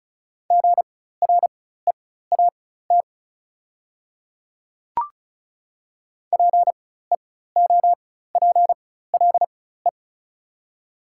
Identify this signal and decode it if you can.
Morse code, heard as a keyed tone, about 36 wpm, spaced as for 16 wpm, 690 Hz. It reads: GREAT PEOPLE
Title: 4x Character Spacing - Top 500 Words - 35wpm